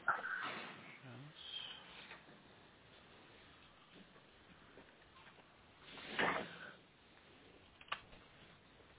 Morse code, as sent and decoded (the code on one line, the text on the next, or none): none